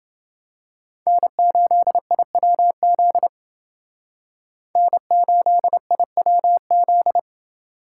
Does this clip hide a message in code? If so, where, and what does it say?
Yes: Morse code from 1.1 to 7.2 s, heard as a keyed tone, about 28 wpm, 700 Hz. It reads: N8IWZ N8IWZ